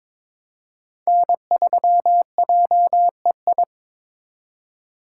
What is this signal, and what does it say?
Morse code, heard as a keyed tone, about 26 wpm, 700 Hz. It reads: N3JEI
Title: Speed Racing - USA 1x3 Call Signs 22wpm